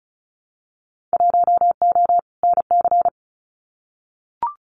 Morse code, heard as a keyed tone, about 36 wpm, 700 Hz. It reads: 1O NC
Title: Winter Field Day Exchanges 35wpm